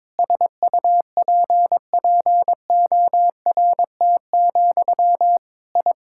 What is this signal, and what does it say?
Morse code, heard as a keyed tone, about 22 wpm, 700 Hz. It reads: SUPPORT, I
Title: Sentences from Top 1000 Words - Review Part 5 - 22wpm